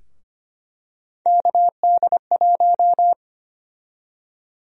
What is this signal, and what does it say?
Morse code, heard as a keyed tone, about 25 wpm, 700 Hz. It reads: KD1
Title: USA Call Sign Prefixes - 25wpm